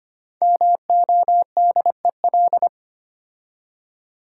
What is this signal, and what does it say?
Morse code, heard as a keyed tone, about 25 wpm, 700 Hz. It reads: MODEL
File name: 5 Letter Words - First 3 Letters Given - 25wpm